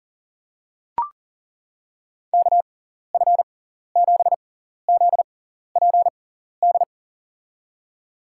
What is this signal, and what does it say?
Morse code, heard as a keyed tone, about 40 wpm, spaced as for 14 wpm, 700 Hz. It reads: KF7ZPD
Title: USA Call Signs - 6 Characters - 6x Character Spacing - 40wpm